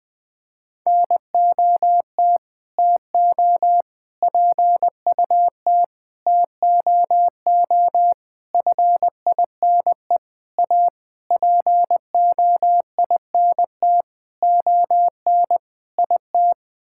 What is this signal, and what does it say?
Morse code, heard as a keyed tone, about 20 wpm, 700 Hz. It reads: NOT TO PUT TOO FINE A POINT ON IT
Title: Sets of 9 Words - 20wpm